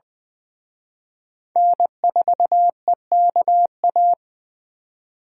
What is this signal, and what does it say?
Morse code, heard as a keyed tone, about 24 wpm, 700 Hz. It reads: N4EKA
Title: Speed Racing - USA 1x3 Call Signs 20wpm